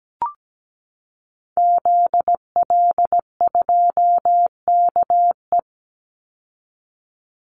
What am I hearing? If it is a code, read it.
Morse code, heard as a keyed tone, about 17 wpm, 700 Hz. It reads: ZL2KE
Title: Call Signs Repeated 2x - 17wpm